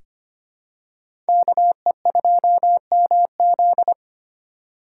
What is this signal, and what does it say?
Morse code, heard as a keyed tone, about 25 wpm, 700 Hz. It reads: KE2MZ
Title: USA Call Signs - 4x Character Spacing - 25wpm